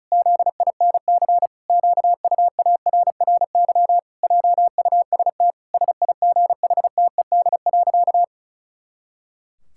Morse code, heard as a keyed tone, about 34 wpm, 700 Hz. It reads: ZINC QUARRY JUST SIGHTED.
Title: Pangram Challenge v2 - 35wpm